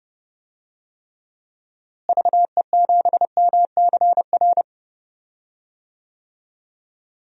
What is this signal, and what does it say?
Morse code, heard as a keyed tone, about 32 wpm, 700 Hz. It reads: VE7MCR
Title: DXCC-100 Call Signs - 6 Characters - Repeated 3x - 30wpm